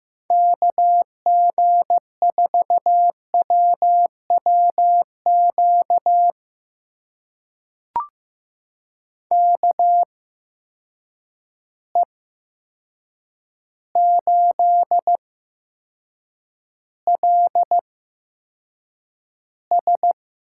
Morse code, heard as a keyed tone, about 15 wpm, spaced as for 7 wpm, 700 Hz. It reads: KG4WWQ KE8LS
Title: USA Call Signs - 6 Characters - 8x Character Spacing - 15wpm